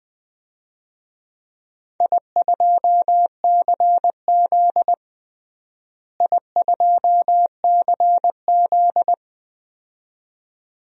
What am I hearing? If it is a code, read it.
Morse code, heard as a keyed tone, about 20 wpm, 700 Hz. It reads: I2CZ I2CZ
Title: DXCC-100 Call Signs - 4 Characters - Repeated 2x - 20wpm